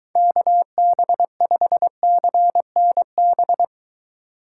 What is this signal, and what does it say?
Morse code, heard as a keyed tone, about 23 wpm, 700 Hz. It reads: KB5CNB